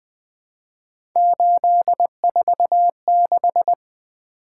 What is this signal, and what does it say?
Morse code, heard as a keyed tone, about 20 wpm, 700 Hz. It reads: 846
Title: Three Letter-Number Combinations - 20wpm